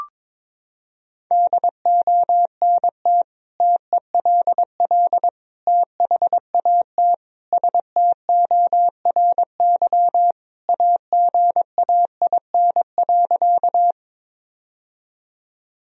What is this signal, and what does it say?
Morse code, heard as a keyed tone, about 22 wpm, 700 Hz. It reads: DONT TELL THAT STORY AGAIN.
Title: Sentences from Top 1000 Words - Review Part 1 - 22wpm